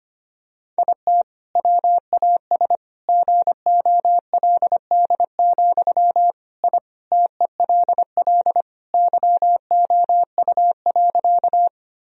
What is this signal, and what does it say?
Morse code, heard as a keyed tone, about 25 wpm, 700 Hz. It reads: IT WAS GOLD, I TELL YOU.
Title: Sentences from Top 1000 Words - Review Part 3 - 25wpm